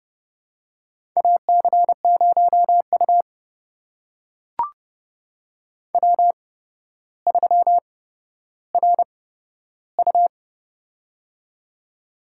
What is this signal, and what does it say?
Morse code, heard as a keyed tone, about 30 wpm, spaced as for 13 wpm, 700 Hz. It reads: AC0U W3RU
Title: USA Call Signs - 8x Character Spacing - 30wpm